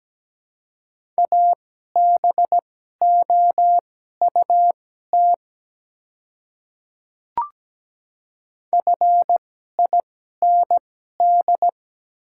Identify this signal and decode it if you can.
Morse code, heard as a keyed tone, about 17 wpm, spaced as for 12 wpm, 700 Hz. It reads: ABOUT FIND
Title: 2x Character Spacing - Top 100 Words - 17wpm